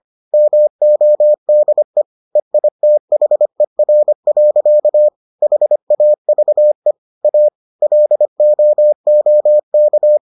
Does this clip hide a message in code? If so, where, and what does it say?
Yes: Morse code from 0.3 to 10.2 s, heard as a keyed tone, about 25 wpm, 600 Hz. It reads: MODE EITHER. HAVE A LOOK